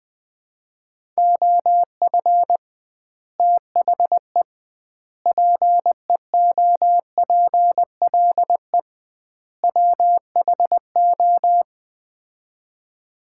Morse code, heard as a keyed tone, about 20 wpm, 700 Hz. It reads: OF THE PEOPLE WHO
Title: 2x Word-Spacing - Sets of 4 Words - 20wpm